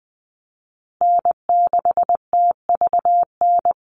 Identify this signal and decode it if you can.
Morse code, heard as a keyed tone, about 20 wpm, 700 Hz. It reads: N6TVN
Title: Call Signs Repeated 2x - 20wpm